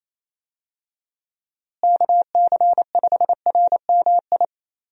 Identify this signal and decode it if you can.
Morse code, heard as a keyed tone, about 28 wpm, 700 Hz. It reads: KC5RMI
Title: USA Call Signs - 6 Characters - Repeated 3x - 28wpm